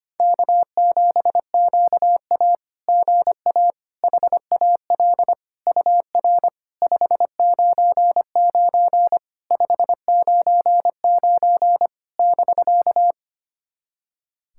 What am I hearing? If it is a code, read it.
Morse code, heard as a keyed tone, about 25 wpm, 690 Hz. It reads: K7QA GA HAL UR 599 599 <BK>
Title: SOTA QSO Practice 25wpm